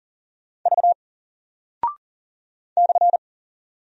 Morse code, heard as a keyed tone, about 39 wpm, 700 Hz. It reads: V /
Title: Single Character (Letter:Number:Punctuation:Prosign) Alt 40wpm